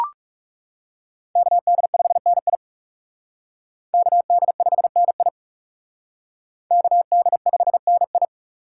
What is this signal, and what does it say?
Morse code, heard as a keyed tone, about 37 wpm, 700 Hz. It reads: KD5NI KD5NI KD5NI
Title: Speed Racing - USA 2x2 Call Signs 30wpm